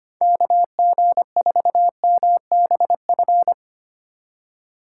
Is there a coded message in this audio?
Yes: Morse code from 0.2 to 3.5 s, heard as a keyed tone, about 32 wpm, 700 Hz. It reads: KG4MBF